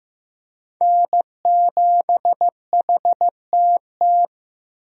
Morse code, heard as a keyed tone, about 15 wpm, 700 Hz. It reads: N7HTT
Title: USA Call Signs - 5 Characters - Repeated 2x - 15wpm